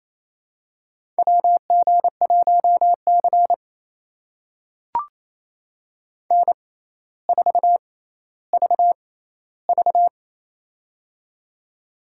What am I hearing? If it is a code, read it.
Morse code, heard as a keyed tone, about 28 wpm, spaced as for 14 wpm, 700 Hz. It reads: WG1C N4VV
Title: USA Call Signs - 4 Characters - 6x Character Spacing - 28wpm